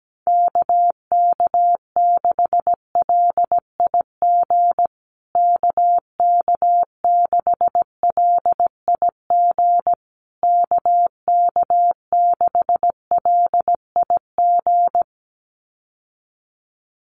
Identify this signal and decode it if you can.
Morse code, heard as a keyed tone, about 17 wpm, 700 Hz. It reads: KK6LIG KK6LIG KK6LIG
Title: USA Call Signs Repeated 3x 17wpm